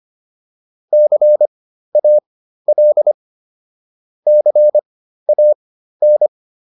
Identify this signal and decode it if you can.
Morse code, heard as a keyed tone, about 25 wpm, spaced as for 13 wpm, 600 Hz. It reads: CAL CAN